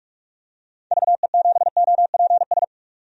Morse code, heard as a keyed tone, about 47 wpm, 700 Hz. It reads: VE7OPS
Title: DXCC-100 Call Signs - 6 Characters - 45wpm